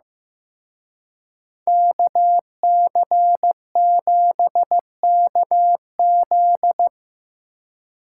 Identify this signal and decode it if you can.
Morse code, heard as a keyed tone, about 17 wpm, 700 Hz. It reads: KC7KZ
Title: Speed Racing - USA Call Signs 15wpm